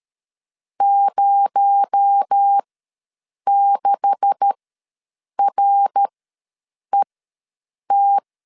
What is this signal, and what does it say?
Morse code, heard as a keyed tone, about 13 wpm, spaced as for 7 wpm, 800 Hz. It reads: 06RET
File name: LEZIONI / 9 286 ws 12wpm 3351